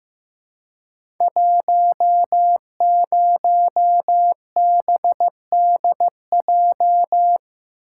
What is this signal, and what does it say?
Morse code, heard as a keyed tone, about 15 wpm, 700 Hz. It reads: 10BDJ
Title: Code Groups - 5 Letter:Numbers - 15wpm